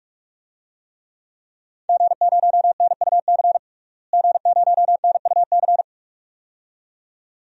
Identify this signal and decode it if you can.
Morse code, heard as a keyed tone, about 46 wpm, 700 Hz. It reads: G0NUC G0NUC
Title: DXCC-100 Call Signs - 5 Characters - Repeated 2x - 45wpm